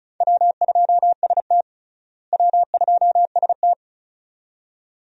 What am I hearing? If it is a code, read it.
Morse code, heard as a keyed tone, about 36 wpm, 700 Hz. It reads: W2ST W2ST
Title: USA Call Signs - 4 Characters - Repeated 2x - 35wpm